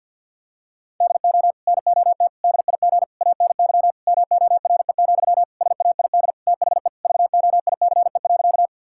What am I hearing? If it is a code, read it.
Morse code, heard as a keyed tone, about 48 wpm, 700 Hz. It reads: DO NOT DIG ANY MORE, SAID THE VOICE.
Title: Sentences from Top 1600 Words 50wpm